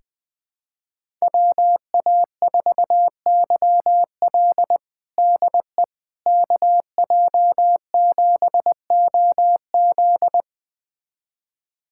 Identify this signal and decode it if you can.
Morse code, heard as a keyed tone, about 20 wpm, 700 Hz. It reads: WA4YL DE KJ7OZ
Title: USA Callsign DE Callsign 20wpm